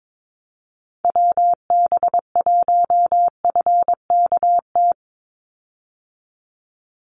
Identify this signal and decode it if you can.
Morse code, heard as a keyed tone, about 22 wpm, 700 Hz. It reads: WB1FKT
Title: USA Call Signs 22wpm